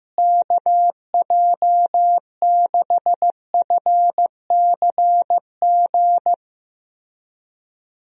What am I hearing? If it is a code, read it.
Morse code, heard as a keyed tone, about 15 wpm, 700 Hz. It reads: KJ6FCG
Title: USA Call Signs - 6 Characters - Repeated 3x - 15wpm